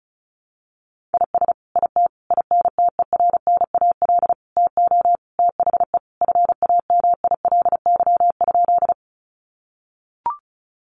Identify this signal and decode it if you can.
Morse code, heard as a keyed tone, about 34 wpm, 700 Hz. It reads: IS IT INTERNAL TO THE FAMILY?
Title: Sentences from Top 1400 Words 35wpm